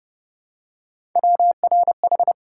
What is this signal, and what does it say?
Morse code, heard as a keyed tone, about 28 wpm, 700 Hz. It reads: WRH